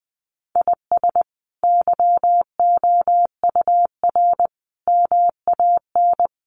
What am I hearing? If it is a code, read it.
Morse code, heard as a keyed tone, about 20 wpm, 700 Hz. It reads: IS YOUR MAN
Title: Sentences from Top 400 Words 20wpm